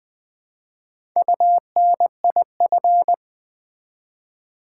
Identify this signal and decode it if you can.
Morse code, heard as a keyed tone, about 20 wpm, 700 Hz. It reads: UNIF
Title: Build A Word - 1 Letter At A Time - 5-Letter Words - 20wpm